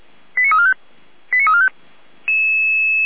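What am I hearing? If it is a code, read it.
Five-tone selective call (ZVEI), heard as quick runs of five tones, tones of about 70 ms. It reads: 89345 89345
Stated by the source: ZVEI five-tone selective call 89130 received on an SDR.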